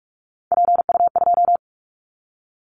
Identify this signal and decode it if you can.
Morse code, heard as a keyed tone, about 46 wpm, 700 Hz. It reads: PU2
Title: Callsign Prefixes for Top 50 DXCC - 45wpm